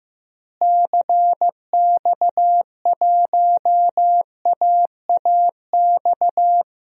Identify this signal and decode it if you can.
Morse code, heard as a keyed tone, about 16 wpm, 700 Hz. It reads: CX1AAX
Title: DXCC-100 Call Signs - 6 Characters - Repeated 3x - 15wpm